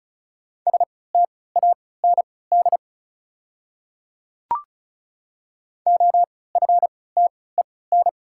Morse code, heard as a keyed tone, about 36 wpm, spaced as for 20 wpm, 690 Hz. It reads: STAND OFTEN
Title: 3x Character Spacing - Top 500 Words - 35wpm